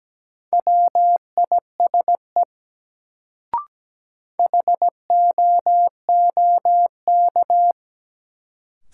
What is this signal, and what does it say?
Morse code, heard as a keyed tone, about 17 wpm, 700 Hz. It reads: WISE HOOK
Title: Sentences from Top 1600 Words - New Word Drill - 17wpm